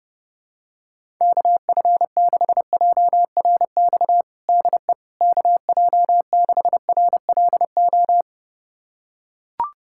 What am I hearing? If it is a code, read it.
Morse code, heard as a keyed tone, about 30 wpm, 700 Hz. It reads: KF6JRX DE KJ6RLO
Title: USA Callsign DE Callsign 30wpm